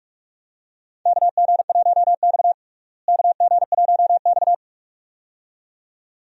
Morse code, heard as a keyed tone, about 46 wpm, 700 Hz. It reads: KG1X KG1X
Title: USA Call Signs - 4 Characters - Repeated 2x - 45wpm